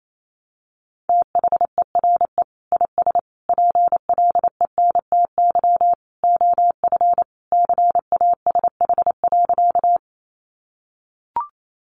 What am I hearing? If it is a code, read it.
Morse code, heard as a keyed tone, about 28 wpm, 700 Hz. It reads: THERE IS PLENTY OF CASH.
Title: Sentences from Top 1300 Words 28wpm